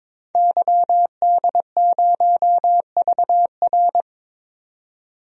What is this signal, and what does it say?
Morse code, heard as a keyed tone, about 22 wpm, 700 Hz. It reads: YD0VR
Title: DXCC-100 Call Signs - 5 Characters - 22wpm